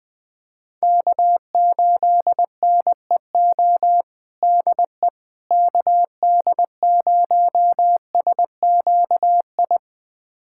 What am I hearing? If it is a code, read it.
Morse code, heard as a keyed tone, about 20 wpm, 700 Hz. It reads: K8NEO DE KD0SQI
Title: USA Callsign DE Callsign 20wpm